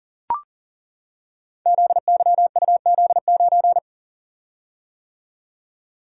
Morse code, heard as a keyed tone, about 40 wpm, 700 Hz. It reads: ZYUZ9
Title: Code Groups - 5 Letter:Numbers - 40wpm